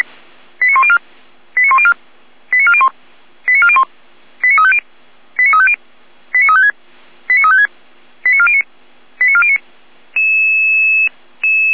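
Five-tone selective call (ZVEI), heard as quick runs of five tones, tones of about 70 ms. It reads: 89104 89104 89401 89401 89350 89350 89356 89356 89409 89409
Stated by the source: ZVEI five-tone call received on an SDR.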